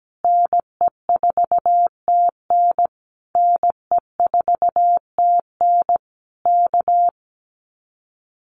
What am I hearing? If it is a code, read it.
Morse code, heard as a keyed tone, about 17 wpm, 700 Hz. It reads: NE4TN NE4TN K
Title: Common QSO Phrases Encore 17wpm